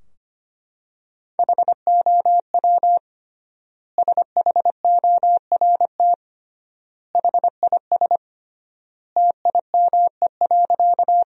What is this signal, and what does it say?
Morse code, heard as a keyed tone, about 25 wpm, 700 Hz. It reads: HOW SHORT HIS TIME.
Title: 3x Word-Spacing - Sentences from Top 400 Words - 25wpm